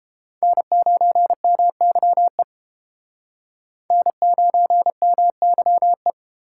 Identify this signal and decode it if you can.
Morse code, heard as a keyed tone, about 31 wpm, 700 Hz. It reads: N9MYE N9MYE